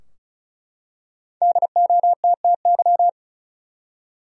Morse code, heard as a keyed tone, about 35 wpm, 700 Hz. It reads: DOTTY
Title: USA Common Names - Max 5 Letters - 35wpm